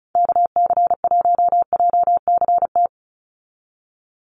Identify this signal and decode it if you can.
Morse code, heard as a keyed tone, about 34 wpm, 700 Hz. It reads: KC1JCT
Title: USA Call Signs 35wpm